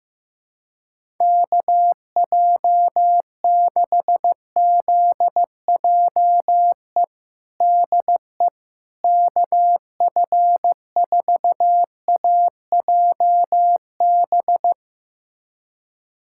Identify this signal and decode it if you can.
Morse code, heard as a keyed tone, about 16 wpm, 700 Hz. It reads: KJ6ZJE DE KF4AJB
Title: USA Callsign DE Callsign 15wpm